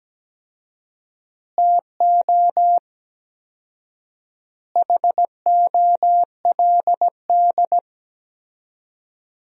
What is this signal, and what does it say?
Morse code, heard as a keyed tone, about 17 wpm, 700 Hz. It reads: TO HOLD
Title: 4x Word-Spacing - Sentences from Top 400 Words - 17wpm